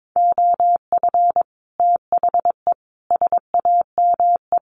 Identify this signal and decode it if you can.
Morse code, heard as a keyed tone, about 22 wpm, 700 Hz. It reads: OF THE SAME
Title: Sets of 3 Words 22wpm